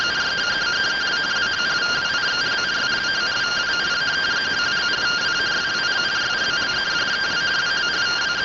Two-tone teleprinter signal, about 170 Hz shift.
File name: rtty rx